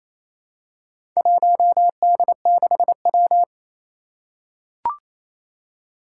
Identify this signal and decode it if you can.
Morse code, heard as a keyed tone, about 28 wpm, 700 Hz. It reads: 1D6W
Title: Four Challenging Letter-Number Combinations 28wpm